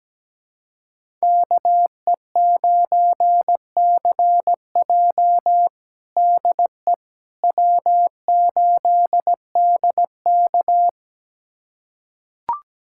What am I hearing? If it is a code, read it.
Morse code, heard as a keyed tone, about 17 wpm, 700 Hz. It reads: KE9CJ DE W8DK
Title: USA Callsign DE Callsign 17wpm